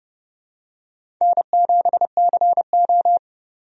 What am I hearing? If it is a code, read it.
Morse code, heard as a keyed tone, about 30 wpm, 700 Hz. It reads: N7CO
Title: USA Call Signs - 4 Characters - 30wpm